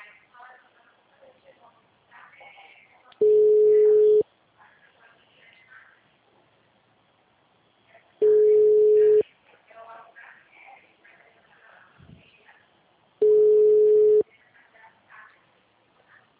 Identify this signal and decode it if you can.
Ringback tone (Europe, 425 Hz), with speech.